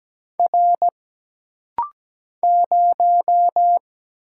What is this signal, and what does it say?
Morse code, heard as a keyed tone, about 17 wpm, 700 Hz. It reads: R 0
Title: Single Letter-Number - Rapid-Fire 17wpm